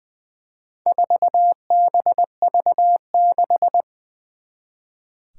Morse code, heard as a keyed tone, about 20 wpm, 700 Hz. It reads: 4BV6
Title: Four Challenging Letter-Number Combinations 20wpm